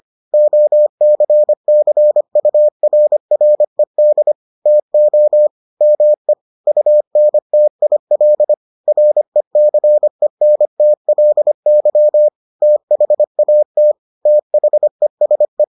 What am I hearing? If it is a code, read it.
Morse code, heard as a keyed tone, about 25 wpm, 600 Hz. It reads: OCCURRED TO ME UNTIL RECENTLY THAT THESE